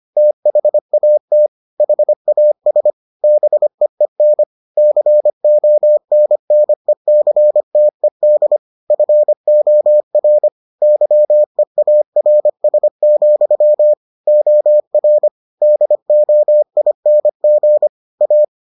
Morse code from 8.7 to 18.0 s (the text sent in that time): FOR YEARS, OR DOING